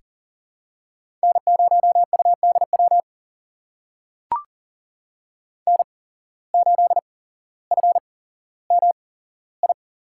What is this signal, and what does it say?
Morse code, heard as a keyed tone, about 40 wpm, spaced as for 17 wpm, 700 Hz. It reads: N0UDW N8FMI
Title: USA Call Signs - 5 Characters - 8x Character Spacing - 40wpm